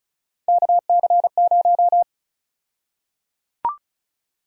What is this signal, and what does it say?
Morse code, heard as a keyed tone, about 36 wpm, 700 Hz. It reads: KC0
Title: USA Call Sign Prefixes - 35wpm